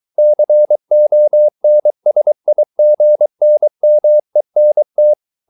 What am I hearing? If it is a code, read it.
Morse code, heard as a keyed tone, about 23 wpm, 600 Hz. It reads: CONSIGNMENT